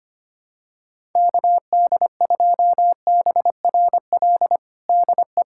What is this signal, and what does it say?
Morse code, heard as a keyed tone, about 25 wpm, 700 Hz. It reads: KD2BRL DE
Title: USA Callsign DE Callsign 25wpm